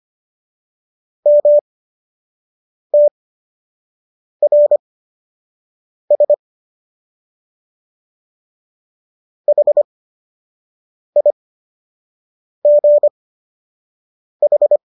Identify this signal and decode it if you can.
Morse code, heard as a keyed tone, about 25 wpm, spaced as for 6 wpm, 590 Hz. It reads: MTRS HIGH
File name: qso012 06